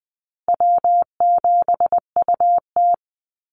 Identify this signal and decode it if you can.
Morse code, heard as a keyed tone, about 20 wpm, 700 Hz. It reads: W7UT
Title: Call Signs Repeated 2x - 20wpm